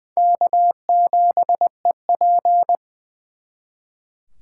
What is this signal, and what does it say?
Morse code, heard as a keyed tone, about 24 wpm, 700 Hz. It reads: K7EP
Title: Speed Racing - USA Call Signs 20wpm